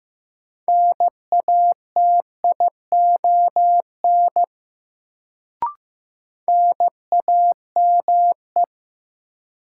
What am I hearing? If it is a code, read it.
Morse code, heard as a keyed tone, about 15 wpm, 700 Hz. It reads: NATION NAME
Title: Sets of 3 Words - New Word Drill - 15wpm